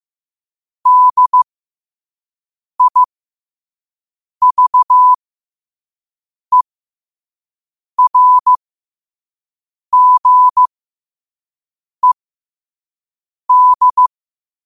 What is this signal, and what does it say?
Morse code, heard as a keyed tone, about 15 wpm, spaced as for 5 wpm, 1000 Hz. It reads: DIVERGED
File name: qotdfiles / 5wpm